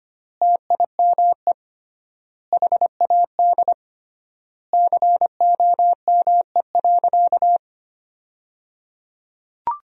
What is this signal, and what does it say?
Morse code, heard as a keyed tone, about 25 wpm, 700 Hz. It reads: TIME HAD COME.
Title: 3x Word-Spacing - Sentences from Top 100 Words - 25wpm